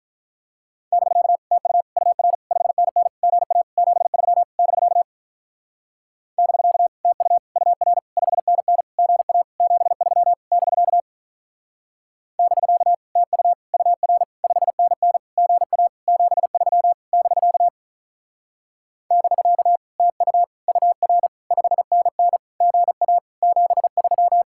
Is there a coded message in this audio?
Yes: Morse code from 0.9 to 24.4 s, heard as a keyed tone, about 45 wpm, 700 Hz. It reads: <BK> TU UR 5NN GA 73 <BK> <BK> TU UR 5NN GA 73 <BK> <BK> TU UR 5NN GA 73 <BK> <BK> TU UR 5NN GA 73